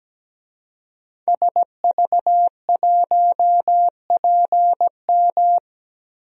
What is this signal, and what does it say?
Morse code, heard as a keyed tone, about 17 wpm, 700 Hz. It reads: SV1PM
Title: DXCC-100 Call Signs - 5 Characters - Repeated 3x - 17wpm